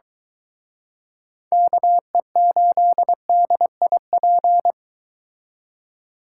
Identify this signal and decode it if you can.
Morse code, heard as a keyed tone, about 23 wpm, 700 Hz. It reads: KE8DIP